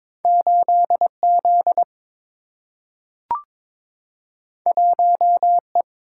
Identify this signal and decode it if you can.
Morse code, heard as a keyed tone, about 22 wpm, 700 Hz. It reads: 8 Z 1 E
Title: Single-Letter Single-Number 22wpm